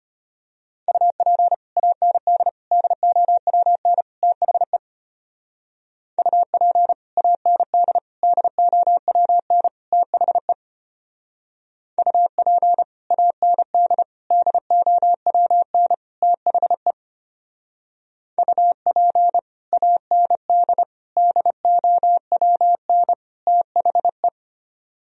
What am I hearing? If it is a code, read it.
Morse code, heard as a keyed tone, about 33 wpm, 700 Hz. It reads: UP AND DOWN THE UP AND DOWN THE UP AND DOWN THE UP AND DOWN THE